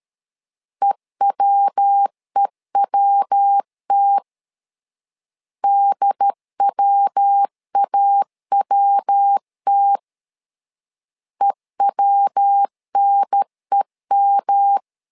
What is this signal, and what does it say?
Morse code, heard as a keyed tone, about 13 wpm, 800 Hz. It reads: EWEWT DWAWT EWNEM
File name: LEZIONI / 3 MWRO / 3 MW 12wpm 1633